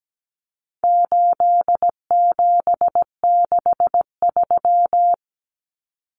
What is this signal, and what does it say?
Morse code, heard as a keyed tone, about 17 wpm, 700 Hz. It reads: 8763
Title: Common Cut Numbers - 4-Digit 17wpm